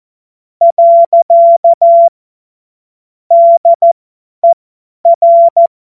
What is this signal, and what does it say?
Morse code, heard as a keyed tone, about 14 wpm, spaced as for 10 wpm, 680 Hz. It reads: . DER
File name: Morsekurs / Witze / Witz 06. September 2022 50 F70